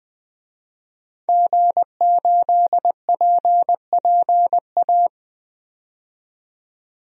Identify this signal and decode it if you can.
Morse code, heard as a keyed tone, about 20 wpm, 700 Hz. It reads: G8PPA
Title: DXCC-100 Call Signs - 5 Characters - 20wpm